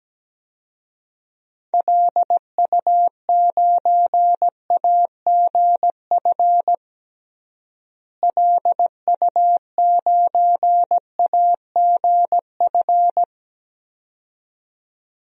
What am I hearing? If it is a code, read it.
Morse code, heard as a keyed tone, about 17 wpm, 700 Hz. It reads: LU9AGF LU9AGF
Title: DXCC-100 Call Signs - 6 Characters - Repeated 2x - 17wpm